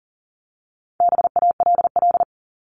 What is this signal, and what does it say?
Morse code, heard as a keyed tone, about 40 wpm, 700 Hz. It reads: BALL
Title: Top 751-1000 Words 40wpm